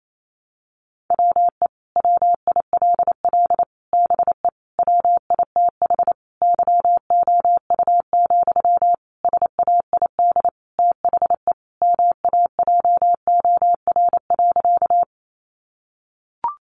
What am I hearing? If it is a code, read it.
Morse code, heard as a keyed tone, about 28 wpm, 700 Hz. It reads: WE WILL BE WITH YOU, SAID THE MAJOR.
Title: Sentences from Top 600 Words 28wpm